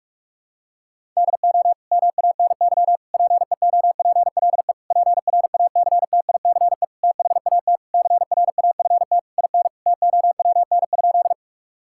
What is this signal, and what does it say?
Morse code, heard as a keyed tone, about 42 wpm, 700 Hz. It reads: DO MANY PEOPLE PRACTICE THAT CRAFT IN TOWN?
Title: Sentences from Top 1000 Words - Review Part 6 - 45wpm